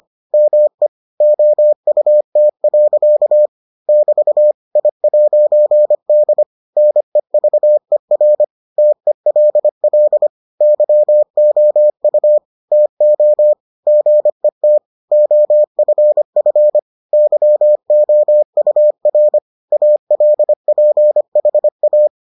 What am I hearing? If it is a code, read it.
Morse code, heard as a keyed tone, about 25 wpm, 600 Hz. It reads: ME OUT. <BT> I'D NEVER TELL YOU TO GET OFF YOUR ALPHA